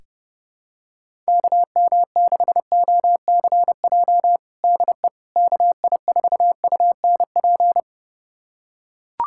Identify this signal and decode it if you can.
Morse code, heard as a keyed tone, about 30 wpm, 700 Hz. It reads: KM6OCJ DE KI4UNP
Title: USA Callsign DE Callsign - 6 Characters - 30wpm